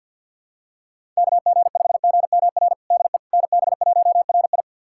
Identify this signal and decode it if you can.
Morse code, heard as a keyed tone, about 51 wpm, 700 Hz. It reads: KG5GMR DE NB1RI
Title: USA Callsign DE Callsign 50wpm